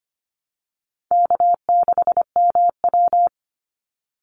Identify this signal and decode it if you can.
Morse code, heard as a keyed tone, about 25 wpm, 700 Hz. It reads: K6MW
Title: NA SOTA Chaser Callsigns 25wpm